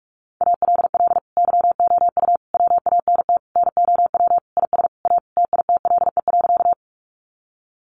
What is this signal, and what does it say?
Morse code, heard as a keyed tone, about 46 wpm, 700 Hz. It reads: ALL YOU WANT NOW IS A TITLE.
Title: Sentences from Top 1200 Words 45wpm